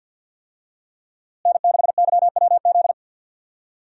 Morse code, heard as a keyed tone, about 51 wpm, 700 Hz. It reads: N6YWZ
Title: USA Call Signs - 5 Characters - Repeated 2x - 50wpm